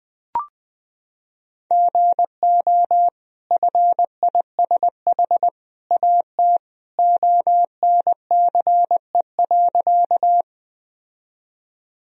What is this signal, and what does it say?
Morse code, heard as a keyed tone, about 20 wpm, 700 Hz. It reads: GO FISH AT ONCE.
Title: Sentences from Top 1000 Words - Review Part 1 - 20wpm